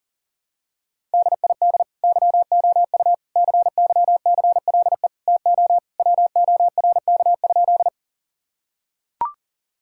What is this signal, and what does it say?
Morse code, heard as a keyed tone, about 37 wpm, 700 Hz. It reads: DID YOU CYCLE TO WORK?
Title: Sentences from Top 1000 Words - Review Part 6 - 40wpm